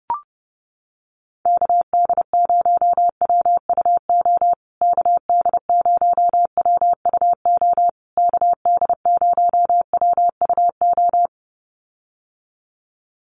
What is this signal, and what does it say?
Morse code, heard as a keyed tone, about 30 wpm, 700 Hz. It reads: KD0WUO KD0WUO KD0WUO
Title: USA Call Signs Repeated 3x 30wpm